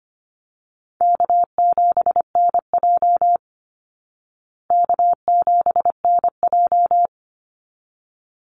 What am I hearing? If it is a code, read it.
Morse code, heard as a keyed tone, about 25 wpm, 700 Hz. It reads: K7NJ K7NJ
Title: Call Signs Repeated 3x - 25wpm